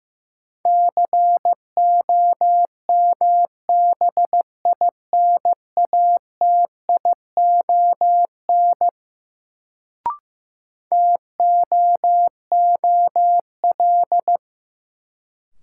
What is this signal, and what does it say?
Morse code, heard as a keyed tone, about 15 wpm, 700 Hz. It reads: COMBINATION TOOL
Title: Sentences from Top 1000 Words - Review Part 6 - New Word Drill - 15wpm